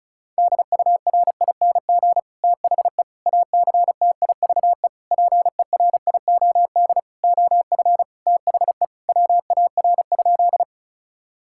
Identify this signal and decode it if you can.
Morse code, heard as a keyed tone, about 34 wpm, 700 Hz. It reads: DURING THE ACTIVE PERIOD OF THE WAR?
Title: Sentences from Top 1000 Words - Review Part 6 - 35wpm